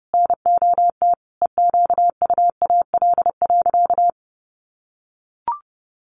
Morse code, heard as a keyed tone, about 30 wpm, 700 Hz. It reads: NOT EQUAL.
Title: Sentences from Top 1400 Words 30wpm